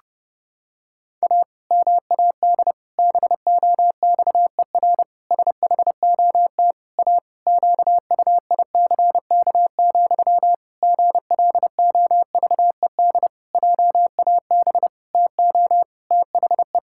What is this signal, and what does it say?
Morse code, heard as a keyed tone, about 30 wpm, 700 Hz. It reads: A MAD BOXER SHOT A QUICK, GLOVED JAB TO THE
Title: Pangram Challenge v2 - 30wpm